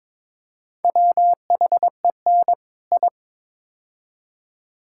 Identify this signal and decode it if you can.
Morse code, heard as a keyed tone, about 22 wpm, 700 Hz. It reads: WHEN I
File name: practice / 22wpm / Build A Phrase - 5-Word Phrases - 22wpm